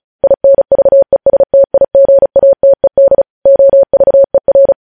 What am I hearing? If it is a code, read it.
Morse code, heard as a keyed tone, about 36 wpm, 550 Hz. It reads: INVESTIGATED OVER
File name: podcast / News headlines-1x-35x35